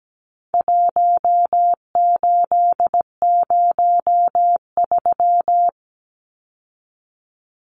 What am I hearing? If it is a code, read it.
Morse code, heard as a keyed tone, about 17 wpm, 700 Hz. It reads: 1803
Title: Numbers - 4-Digit 17wpm